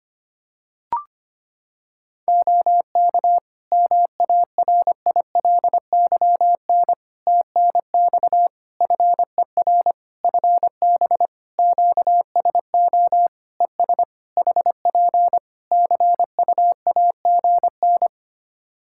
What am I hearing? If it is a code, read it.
Morse code, heard as a keyed tone, about 25 wpm, 700 Hz. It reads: OK MARILYN TNX FER FB QSO ES HP CUAGN
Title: Practice QSO - Exchange 4 - Ending - 25wpm